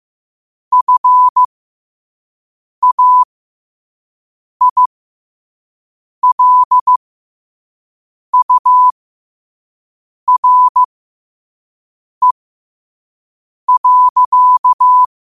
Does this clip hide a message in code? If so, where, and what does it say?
Yes: Morse code from 0.7 to 15.0 s, heard as a keyed tone, about 15 wpm, spaced as for 5 wpm, 1000 Hz. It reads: FAILURE.